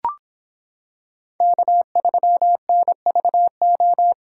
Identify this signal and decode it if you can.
Morse code, heard as a keyed tone, about 26 wpm, 700 Hz. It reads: K3NVO